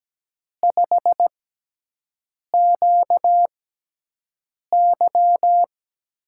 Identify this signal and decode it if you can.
Morse code, heard as a keyed tone, about 17 wpm, spaced as for 6 wpm, 700 Hz. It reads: 5QY
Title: USA Call Signs - 6 Characters - 6x Character Spacing - 17wpm